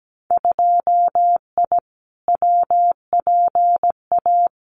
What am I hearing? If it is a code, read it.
Morse code, heard as a keyed tone, about 17 wpm, 700 Hz. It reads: 2I WPA
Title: Winter Field Day Exchanges 17wpm